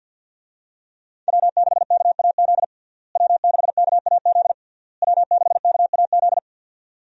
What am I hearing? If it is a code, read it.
Morse code, heard as a keyed tone, about 48 wpm, 700 Hz. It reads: W6KAZ W6KAZ W6KAZ
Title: USA Call Signs - 5 Characters - Repeated 3x - 50wpm